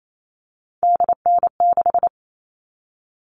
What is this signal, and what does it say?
Morse code, heard as a keyed tone, about 28 wpm, 700 Hz. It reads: DN6
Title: All Cut Numbers - 3-Digit 28wpm